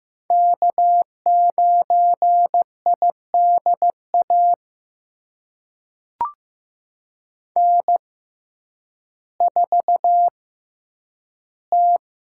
Morse code, heard as a keyed tone, about 15 wpm, spaced as for 9 wpm, 700 Hz. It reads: K9IDA N4T
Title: USA Call Signs - 6x Character Spacing - 15wpm